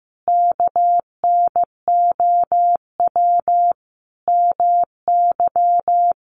Morse code, heard as a keyed tone, about 15 wpm, 700 Hz. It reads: KNOW MY
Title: Sentences from Top 1200 Words 15wpm